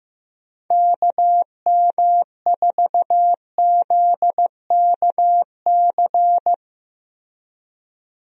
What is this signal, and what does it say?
Morse code, heard as a keyed tone, about 15 wpm, 700 Hz. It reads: KM4ZKC
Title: USA Call Signs - 6 Characters - 15wpm